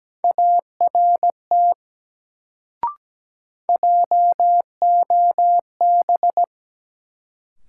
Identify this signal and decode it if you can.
Morse code, heard as a keyed tone, about 17 wpm, 700 Hz. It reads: ART JOB
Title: Sentences from Top 600 Words - New Word Drill - 17wpm